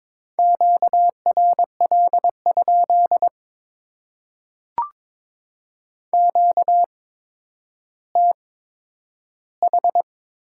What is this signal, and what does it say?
Morse code, heard as a keyed tone, about 22 wpm, spaced as for 11 wpm, 700 Hz. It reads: QRL? QTH
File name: Top 90 QSO Elements - 8x Character Spacing - 22wpm